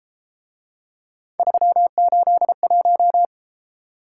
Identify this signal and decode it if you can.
Morse code, heard as a keyed tone, about 32 wpm, 700 Hz. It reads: 381